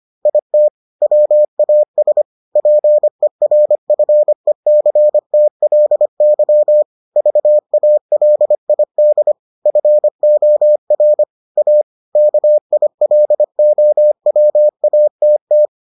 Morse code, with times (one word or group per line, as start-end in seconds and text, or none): 0.3-0.7 IT
1.0-2.2 WAS
2.6-6.8 PERFECTLY
7.2-9.3 VALID
9.7-11.2 FOR
11.6-11.8 A
12.2-15.7 KILOWATT